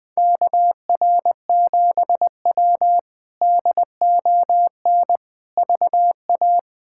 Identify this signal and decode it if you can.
Morse code, heard as a keyed tone, about 20 wpm, 700 Hz. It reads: KR7W DON VA
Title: K1USN SST QSO Practice - 20wpm